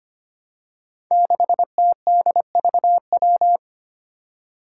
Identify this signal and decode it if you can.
Morse code, heard as a keyed tone, about 25 wpm, 700 Hz. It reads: 6TDVW
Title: Five Challenging Letter-Number Combinations 25wpm